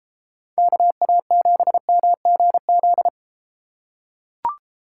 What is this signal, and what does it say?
Morse code, heard as a keyed tone, about 33 wpm, 700 Hz. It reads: KA7MGZ